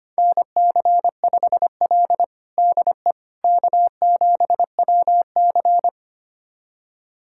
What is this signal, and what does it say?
Morse code, heard as a keyed tone, about 25 wpm, 700 Hz. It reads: NC5L DE K7WC
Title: USA Callsign DE Callsign - 4 Characters - 25wpm